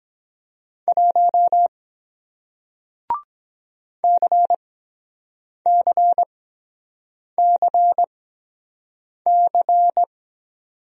Morse code, heard as a keyed tone, about 24 wpm, spaced as for 7 wpm, 700 Hz. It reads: 1 CCCC